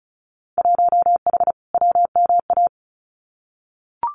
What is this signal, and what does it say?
Morse code, heard as a keyed tone, about 34 wpm, 700 Hz. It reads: 1H WMA
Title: Winter Field Day Exchanges 35wpm